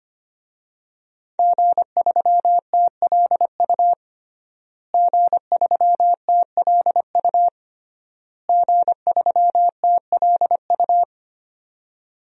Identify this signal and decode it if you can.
Morse code, heard as a keyed tone, about 25 wpm, 700 Hz. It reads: G3TLU G3TLU G3TLU
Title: DXCC-100 Call Signs - 5 Characters - Repeated 3x - 25wpm